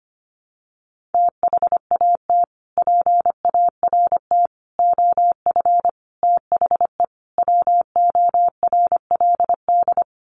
Morse code, heard as a keyed tone, about 25 wpm, 700 Hz. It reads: THAT PART OF THE WORLD
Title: Sets of 5 Words Encore2 25wpm